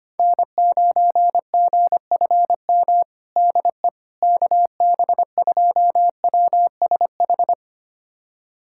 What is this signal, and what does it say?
Morse code, heard as a keyed tone, about 25 wpm, 700 Hz. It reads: N9GFM DE KB2WSH
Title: USA Callsign DE Callsign 25wpm